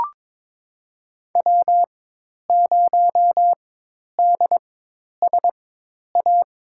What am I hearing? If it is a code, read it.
Morse code, heard as a keyed tone, about 21 wpm, spaced as for 10 wpm, 700 Hz. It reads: W0DSA
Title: USA Call Signs - 4x Character Spacing - 22wpm